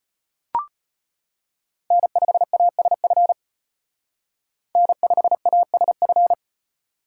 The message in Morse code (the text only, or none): N5ASF N5ASF